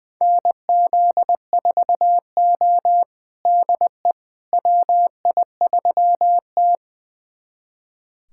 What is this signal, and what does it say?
Morse code, heard as a keyed tone, about 20 wpm, 700 Hz. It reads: NZ4O DE WI3T
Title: USA Callsign DE Callsign - 4 Characters - 20wpm